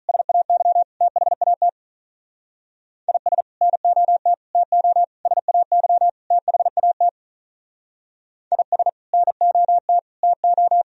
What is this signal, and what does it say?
Morse code, heard as a keyed tone, about 43 wpm, 700 Hz. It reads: SAY THAT IS NOT TO SAY THAT IS NOT TO
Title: Speed-Racing - Sets of 5 Words 35wpm